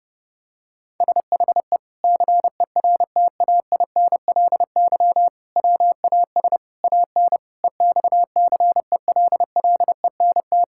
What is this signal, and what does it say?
Morse code, heard as a keyed tone, about 32 wpm, 700 Hz. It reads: SHE CERTAINLY WAS AN EXCELLENT
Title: Sentences from Top 1000 Words - Review Part 6 - 30wpm